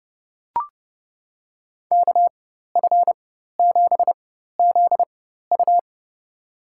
Morse code, heard as a keyed tone, about 30 wpm, spaced as for 14 wpm, 700 Hz. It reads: KF7ZU
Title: USA Call Signs - 4x Character Spacing - 30wpm